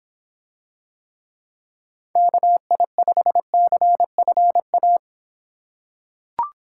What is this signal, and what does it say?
Morse code, heard as a keyed tone, about 25 wpm, 700 Hz. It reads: KI5CFA